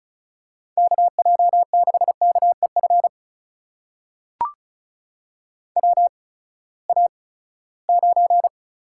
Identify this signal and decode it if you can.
Morse code, heard as a keyed tone, about 36 wpm, spaced as for 20 wpm, 700 Hz. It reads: KJ6KEF WA9
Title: USA Call Signs - 6 Characters - 8x Character Spacing - 35wpm